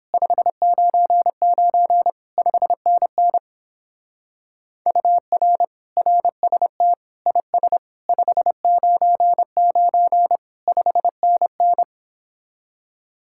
Morse code, heard as a keyed tone, about 31 wpm, 700 Hz. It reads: 599 5NN UR RST IS 599 5NN